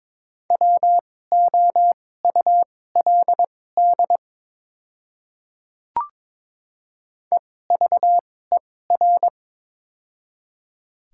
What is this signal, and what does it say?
Morse code, heard as a keyed tone, about 22 wpm, spaced as for 16 wpm, 700 Hz. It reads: WOULD EVER
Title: 2x Character Spacing - Top 500 Words - 22wpm